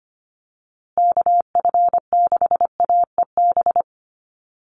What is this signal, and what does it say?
Morse code, heard as a keyed tone, about 25 wpm, 700 Hz. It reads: KF6AEB
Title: USA Call Signs Repeated 3x 25wpm